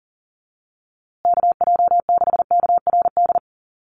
Morse code, heard as a keyed tone, about 40 wpm, 700 Hz. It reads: KJ6KRD
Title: USA Call Signs Repeated 2x 40wpm